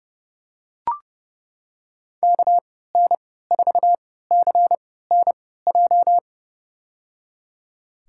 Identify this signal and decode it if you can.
Morse code, heard as a keyed tone, about 30 wpm, spaced as for 17 wpm, 700 Hz. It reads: KN4CNJ
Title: USA Call Signs - 6 Characters - 3x Character Spacing - 30wpm